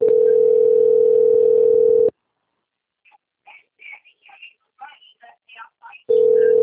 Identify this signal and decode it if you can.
Ringback tone (North America), 440 + 480 Hz, with speech.
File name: out- - - - - .11254